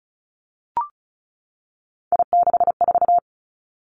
Morse code, heard as a keyed tone, about 36 wpm, 700 Hz. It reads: I64
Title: Three Challenging Letter-Number Combinations 35wpm